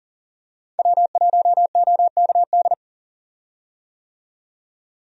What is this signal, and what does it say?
Morse code, heard as a keyed tone, about 40 wpm, 700 Hz. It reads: W1OKD
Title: USA Call Signs - 5 Characters - 40wpm